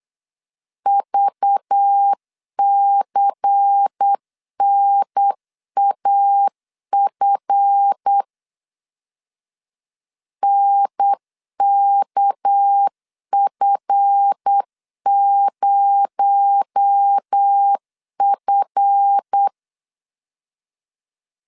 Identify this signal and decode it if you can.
Morse code, heard as a keyed tone, about 8 wpm, 800 Hz. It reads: VCNAF NKF0F
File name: LEZIONI / 7 GF9Z / 7 GF 8wpm 3108